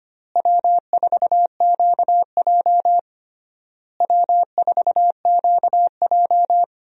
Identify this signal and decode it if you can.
Morse code, heard as a keyed tone, about 25 wpm, 700 Hz. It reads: W4QJ W4QJ
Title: USA Call Signs - 4 Characters - Repeated 3x - 25wpm